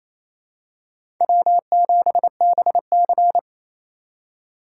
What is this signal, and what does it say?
Morse code, heard as a keyed tone, about 27 wpm, 700 Hz. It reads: W7BC
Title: USA Call Signs - 4 Characters - 4x Character Spacing - 28wpm